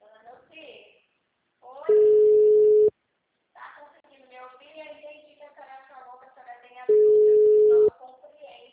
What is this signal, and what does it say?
Ringback tone (Europe, 425 Hz), with speech.